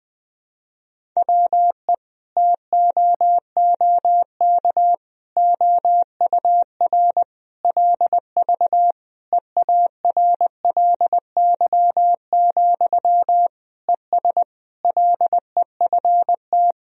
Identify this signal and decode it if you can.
Morse code, heard as a keyed tone, about 20 wpm, 700 Hz. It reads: WE TOOK OUR LV EARLY, ES LEFT
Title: Sentences from Top 400 Words and Abbreviations 20wpm